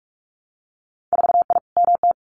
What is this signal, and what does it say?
Morse code, heard as a keyed tone, about 46 wpm, 690 Hz. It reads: 4I MT
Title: Winter Field Day Exchanges Encore 45wpm